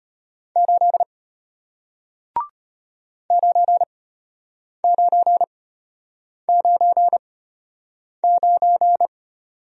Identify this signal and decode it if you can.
Morse code, heard as a keyed tone, about 33 wpm, spaced as for 8 wpm, 700 Hz. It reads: 8 9999